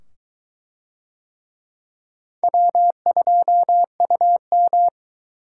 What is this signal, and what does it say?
Morse code, heard as a keyed tone, about 22 wpm, 700 Hz. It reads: W2UM